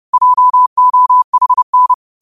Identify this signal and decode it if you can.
Morse code, heard as a keyed tone, about 30 wpm, 1000 Hz. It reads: JOHN
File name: qotdfiles / 30wpm